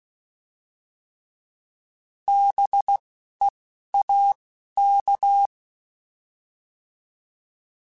Morse code, heard as a keyed tone, about 16 wpm, 800 Hz. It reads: B E A K